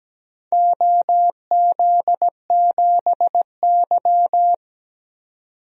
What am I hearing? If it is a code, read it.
Morse code, heard as a keyed tone, about 17 wpm, 700 Hz. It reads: OZ7Y
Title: DXCC-100 Call Signs - 4 Characters - 17wpm